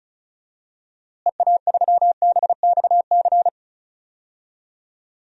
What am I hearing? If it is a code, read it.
Morse code, heard as a keyed tone, about 36 wpm, 700 Hz. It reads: EA3BXC
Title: DXCC-100 Call Signs - 6 Characters - Repeated 3x - 35wpm